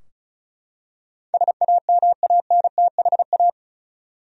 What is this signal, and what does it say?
Morse code, heard as a keyed tone, about 36 wpm, 700 Hz. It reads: SAMANTHA
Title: USA Common Names - 35wpm